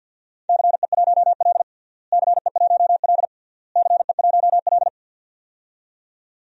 Morse code, heard as a keyed tone, about 50 wpm, 700 Hz. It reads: CE1L CE1L CE1L
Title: DXCC-100 Call Signs - 4 Characters - Repeated 3x - 50wpm